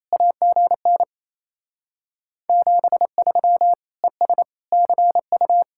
Morse code, heard as a keyed tone, about 35 wpm, 700 Hz. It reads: AGN 73 ES CU
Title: Speed-Racing Common QSO Phrases - 28wpm